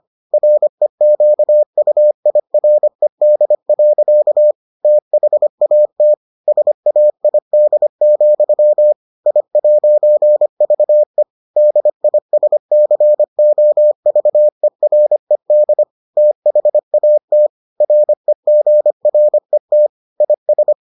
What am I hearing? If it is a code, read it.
Morse code, heard as a keyed tone, about 25 wpm, 600 Hz. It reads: REQUIRED. THAT SAID, I'VE DISCOVERED THAT REGRET IS